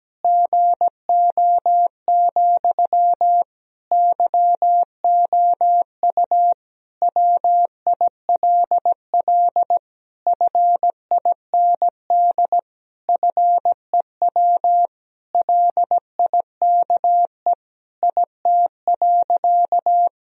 Morse code, heard as a keyed tone, about 17 wpm, 700 Hz. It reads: GO, YOU WILL FIND FEW LIKE IT.
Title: Sentences from Top 1000 Words - Review Part 1 - 17wpm